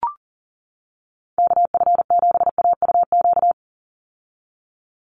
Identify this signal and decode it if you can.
Morse code, heard as a keyed tone, about 40 wpm, 700 Hz. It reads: KF7AUQ
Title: USA Call Signs 40wpm